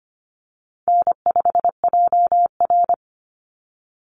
Morse code, heard as a keyed tone, about 25 wpm, 700 Hz. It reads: N5JR
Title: Easy Call Signs Repeated 3x - 25wpm